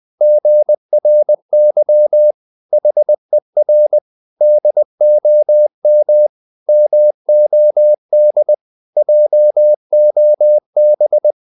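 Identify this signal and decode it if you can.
Morse code, heard as a keyed tone, about 20 wpm, 600 Hz. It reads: GRY HER DOM MOD JOB
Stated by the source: Morse code practice of short Danish words at 20 wpm, from CW med Gnister.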